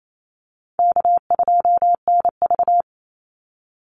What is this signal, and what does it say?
Morse code, heard as a keyed tone, about 28 wpm, 700 Hz. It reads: K2NV
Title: Easy Call Signs - 28wpm